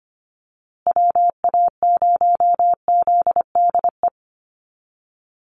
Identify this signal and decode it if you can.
Morse code, heard as a keyed tone, about 25 wpm, 700 Hz. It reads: WA0ZDE
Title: USA Call Signs Repeated 3x 25wpm